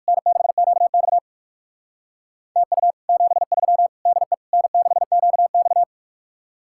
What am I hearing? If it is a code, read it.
Morse code, heard as a keyed tone, about 63 wpm, spaced as for 55 wpm, 700 Hz. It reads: N6QX TU 73 DE N6QX
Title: Speed-Racing Common QSO Phrases Encore - 45wpm